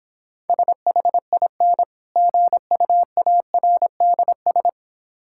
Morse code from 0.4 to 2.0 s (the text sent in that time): SHIN